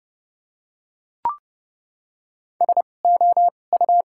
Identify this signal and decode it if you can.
Morse code, heard as a keyed tone, about 30 wpm, spaced as for 22 wpm, 700 Hz. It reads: SOU
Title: 2x Character Spacing - Top 500 Words - 30wpm